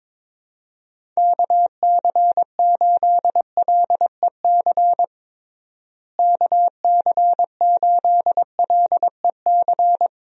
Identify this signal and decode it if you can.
Morse code, heard as a keyed tone, about 22 wpm, 700 Hz. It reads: KC8LEC KC8LEC
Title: USA Call Signs - 6 Characters - Repeated 3x - 22wpm